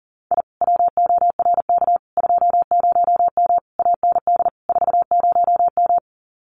Morse code, heard as a keyed tone, about 40 wpm, 700 Hz. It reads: I WORK 20M AND 40M
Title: Common QSO Phrases Encore 40wpm